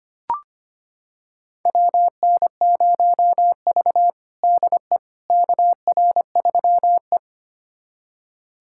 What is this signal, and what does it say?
Morse code, heard as a keyed tone, about 25 wpm, 700 Hz. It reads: WN0V DE KR3E
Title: USA Callsign DE Callsign - 4 Characters - 25wpm